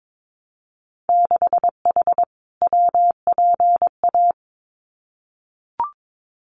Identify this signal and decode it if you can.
Morse code, heard as a keyed tone, about 22 wpm, 700 Hz. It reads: 6H WPA
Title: Winter Field Day Exchanges Encore 22wpm